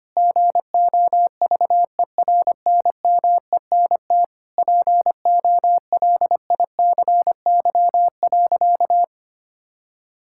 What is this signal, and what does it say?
Morse code, heard as a keyed tone, about 25 wpm, 700 Hz. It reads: GOVERNMENT POLICY.
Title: Sentences from Top 1000 Words - Review Part 5 - 25wpm